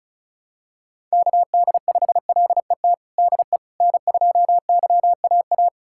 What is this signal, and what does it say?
Morse code, heard as a keyed tone, about 36 wpm, 700 Hz. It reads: KD5LET DE N2YAA
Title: USA Callsign DE Callsign 35wpm